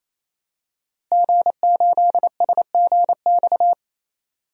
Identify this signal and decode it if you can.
Morse code, heard as a keyed tone, about 28 wpm, 700 Hz. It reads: G8SGX
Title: DXCC-100 Call Signs - 5 Characters - Repeated 3x - 28wpm